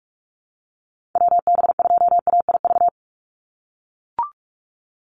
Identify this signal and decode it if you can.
Morse code, heard as a keyed tone, about 44 wpm, 700 Hz. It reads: WB2AIV
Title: Call Signs - 45wpm